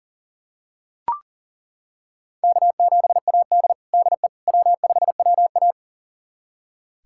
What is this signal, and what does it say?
Morse code, heard as a keyed tone, about 40 wpm, 700 Hz. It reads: K7AD DE W5WA
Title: USA Callsign DE Callsign - 4 Characters - 40wpm